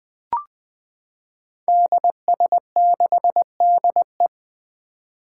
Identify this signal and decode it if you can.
Morse code, heard as a keyed tone, about 20 wpm, 700 Hz. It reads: DS6DE
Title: Five Challenging Letter-Number Combinations 20wpm